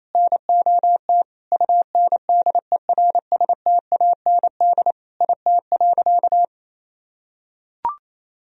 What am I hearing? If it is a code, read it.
Morse code, heard as a keyed tone, about 28 wpm, 690 Hz. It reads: NOT UNDERSTAND IT.
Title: Sentences from Top 1000 Words - Review Part 3 - 28wpm